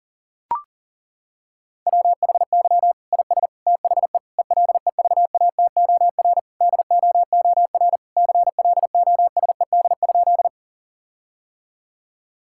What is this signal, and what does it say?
Morse code, heard as a keyed tone, about 40 wpm, 700 Hz. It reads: WHY IS THE ELEVATOR DOOR CLOSED?
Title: Sentences from Top 1600 Words 40wpm